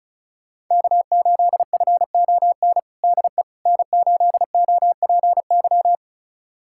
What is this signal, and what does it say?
Morse code, heard as a keyed tone, about 36 wpm, 700 Hz. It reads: K8FON DE N8OPY
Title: USA Callsign DE Callsign - 5 Characters - 35wpm